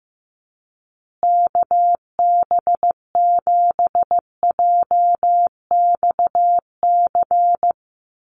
Morse code, heard as a keyed tone, about 15 wpm, 700 Hz. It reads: KB7JXC
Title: USA Call Signs Repeated 3x 15wpm